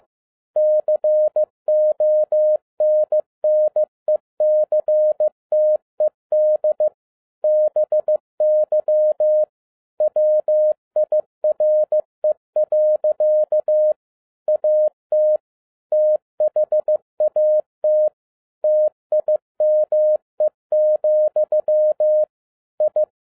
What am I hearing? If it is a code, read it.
Morse code, heard as a keyed tone, about 15 wpm, 610 Hz. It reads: CONNECTED BY WIRE. AT THAT TIME, I